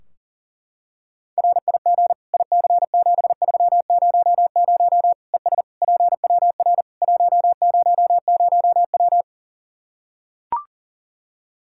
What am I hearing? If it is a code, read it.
Morse code, heard as a keyed tone, about 40 wpm, 700 Hz. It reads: RIG IC7300 ES PWR 100W
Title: Practice QSO - Exchange 2 - Info - 40wpm